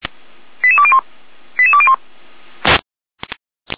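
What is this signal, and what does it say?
Five-tone selective call (ZVEI), heard as quick runs of five tones, tones of about 70 ms. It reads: 88201 88201
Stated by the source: ZVEI five-tone selective call received on SDR.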